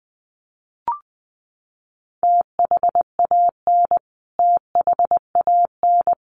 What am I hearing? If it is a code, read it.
Morse code, heard as a keyed tone, about 20 wpm, 700 Hz. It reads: THAN THAN
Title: Top 500 Words Repeated 20wpm